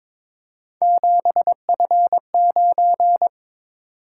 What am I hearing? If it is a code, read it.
Morse code, heard as a keyed tone, about 23 wpm, 700 Hz. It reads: 7F9
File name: practice / 22wpm / Three Letter-Number Combinations - 22wpm